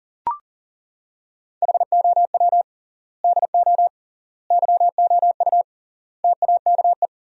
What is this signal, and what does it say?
Morse code, heard as a keyed tone, about 37 wpm, 700 Hz. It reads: HOW DO YOU TAKE
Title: 3x Word-Spacing - Sentences from Top 200 Words - 40wpm